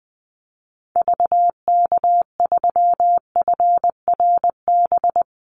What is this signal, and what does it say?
Morse code, heard as a keyed tone, about 20 wpm, 700 Hz. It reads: VK3FRB
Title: USA Call Signs 20wpm